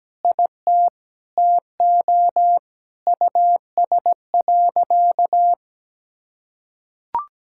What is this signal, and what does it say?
Morse code, heard as a keyed tone, about 17 wpm, 700 Hz. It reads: IT TO US.
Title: Sentences from Top 1000 Words - Review Part 5 - 17wpm